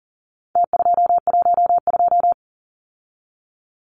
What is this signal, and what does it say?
Morse code, heard as a keyed tone, about 40 wpm, 700 Hz. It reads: T212
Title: Numbers - Contest Style - 4-Digit 40wpm